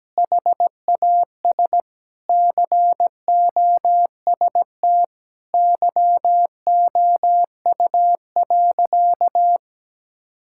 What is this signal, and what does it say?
Morse code, heard as a keyed tone, about 17 wpm, 700 Hz. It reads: HAS COST YOU.
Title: Sentences from Top 1000 Words - Review Part 5 - 17wpm